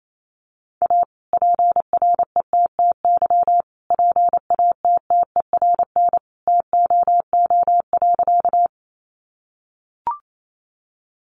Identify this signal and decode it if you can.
Morse code, heard as a keyed tone, about 28 wpm, 700 Hz. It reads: A PRETTY PATTERN TOO.
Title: Sentences from Top 1000 Words 28wpm